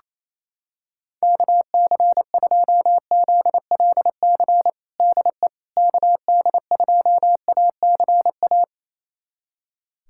Morse code, heard as a keyed tone, about 28 wpm, 700 Hz. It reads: KC2ZLC DE KD2ACA
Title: USA Callsign DE Callsign 28wpm